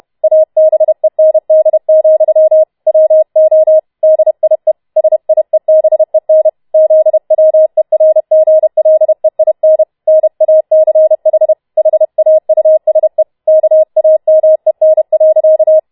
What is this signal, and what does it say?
Morse code, heard as a keyed tone, about 31 wpm, 630 Hz. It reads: ABEND, WO DIE SIEBEN ZWERGLEIN NACH HAUSE KAMEN.